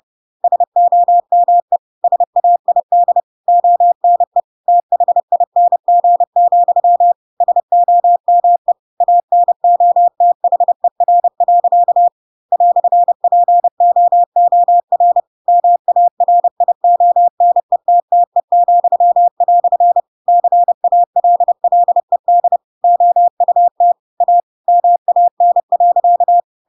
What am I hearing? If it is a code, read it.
Morse code, heard as a keyed tone, about 30 wpm, 700 Hz. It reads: SOME SAID ONE THING, SOME ANOTHER. "POOR MARIONETTE," CALLED OUT A MAN.
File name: Pinocchio / Pinocchio ch3 30 30 ik1hge